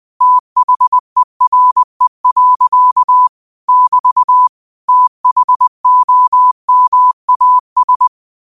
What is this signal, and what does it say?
Morse code, heard as a keyed tone, about 20 wpm, 1000 Hz. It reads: THERE. <BT> THOMAS
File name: 20wpm